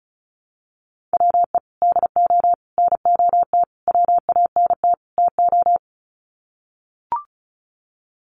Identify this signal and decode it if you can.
Morse code, heard as a keyed tone, about 34 wpm, 700 Hz. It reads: WE DO NOT WANT TO
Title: Sets of 5 Words 35wpm